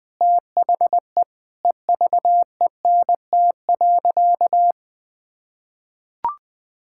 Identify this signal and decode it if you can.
Morse code, heard as a keyed tone, about 21 wpm, 700 Hz. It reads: THE EVENT.
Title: Sentences with 5-Word Phrases - Spoken Beginning - 20wpm